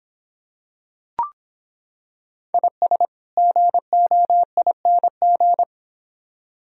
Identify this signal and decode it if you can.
Morse code, heard as a keyed tone, about 25 wpm, 700 Hz. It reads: IS GOING